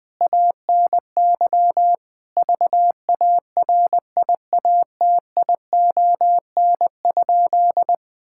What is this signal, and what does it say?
Morse code, heard as a keyed tone, about 20 wpm, 700 Hz. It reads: ANY VARIATION?
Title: Sentences from Top 1600 Words 20wpm